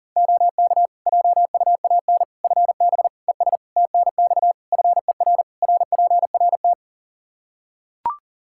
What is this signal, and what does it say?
Morse code, heard as a keyed tone, about 40 wpm, 700 Hz. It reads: OK JUAN FB ES TNX FER RPRT
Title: K1USN SST QSO Practice 40wpm